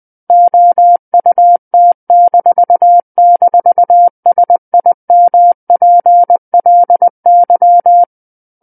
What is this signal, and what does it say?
Morse code, heard as a keyed tone, about 20 wpm, 700 Hz. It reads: OUT--SIMPLY
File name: 20 WPM